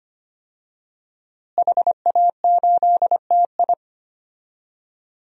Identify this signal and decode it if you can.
Morse code, heard as a keyed tone, about 25 wpm, 700 Hz. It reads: HA8TI
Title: DXCC-100 Call Signs - 5 Characters - Repeated 3x - 25wpm